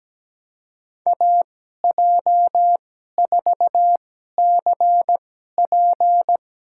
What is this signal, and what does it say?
Morse code, heard as a keyed tone, about 17 wpm, spaced as for 12 wpm, 700 Hz. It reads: AJ4CP
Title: USA Call Signs - 5 Characters - 2x Character Spacing - 17wpm